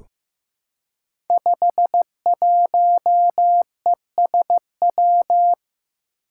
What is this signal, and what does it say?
Morse code, heard as a keyed tone, about 15 wpm, 700 Hz. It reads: 51ESW
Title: Five Challenging Letter-Number Combinations 15wpm